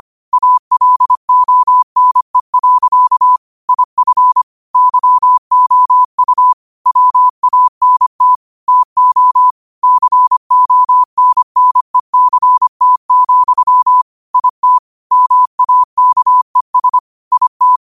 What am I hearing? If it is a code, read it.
Morse code, heard as a keyed tone, about 25 wpm, 1000 Hz. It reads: ALONE. IF YOU WANT TO CONNECT, IT MAKES IT
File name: qotdfiles / 25wpm